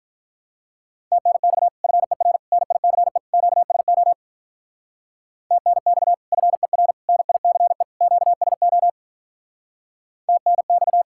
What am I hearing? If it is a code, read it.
Morse code, heard as a keyed tone, about 47 wpm, 700 Hz. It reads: TNX FER NICE QSO TNX FER NICE QSO TNX